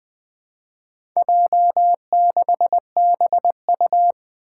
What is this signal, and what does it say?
Morse code, heard as a keyed tone, about 20 wpm, 700 Hz. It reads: J6BU
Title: Four Challenging Letter-Number Combinations 20wpm